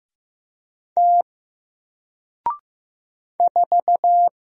Morse code, heard as a keyed tone, about 15 wpm, 700 Hz. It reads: T 4
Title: Single Letter-Number Alt 15wpm